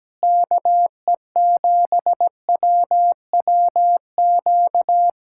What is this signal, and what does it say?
Morse code, heard as a keyed tone, about 17 wpm, 700 Hz. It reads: KE7WWQ
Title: USA Callsign DE Callsign 17wpm